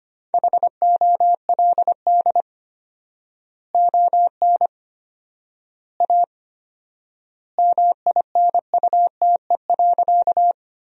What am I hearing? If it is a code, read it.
Morse code, heard as a keyed tone, about 25 wpm, 700 Hz. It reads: HOLD ON A MINUTE.
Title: 4x Word-Spacing - Sentences from Top 500 Words - 25wpm